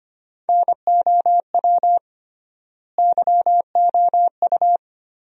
Morse code, heard as a keyed tone, about 25 wpm, 700 Hz. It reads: NOW YOU
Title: 3x Word-Spacing - Sentences from Top 100 Words - 25wpm